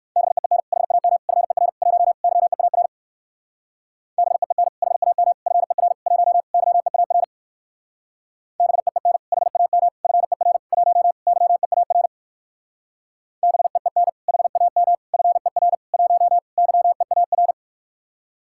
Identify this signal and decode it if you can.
Morse code, heard as a keyed tone, about 62 wpm, 700 Hz. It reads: BEEN HAM FER 1 YEAR BEEN HAM FER 1 YEAR BEEN HAM FER 1 YEAR BEEN HAM FER 1 YEAR